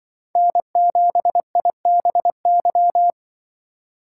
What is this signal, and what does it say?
Morse code, heard as a keyed tone, about 24 wpm, 700 Hz. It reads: N7IBY
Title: Speed Racing - USA 1x3 Call Signs 20wpm